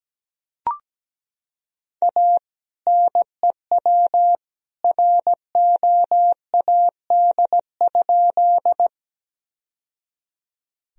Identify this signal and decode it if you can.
Morse code, heard as a keyed tone, about 17 wpm, 700 Hz. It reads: A NEW ROAD?
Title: Sentences from Top 1000 Words - Review Part 2 - 17wpm